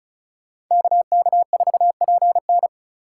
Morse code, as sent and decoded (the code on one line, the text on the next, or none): -.- -.- ....- .--. -.
KK4PN